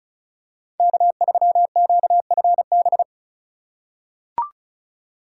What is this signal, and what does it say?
Morse code, heard as a keyed tone, about 36 wpm, 700 Hz. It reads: K3QFB
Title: USA Call Signs - 5 Characters - 2x Character Spacing - 35wpm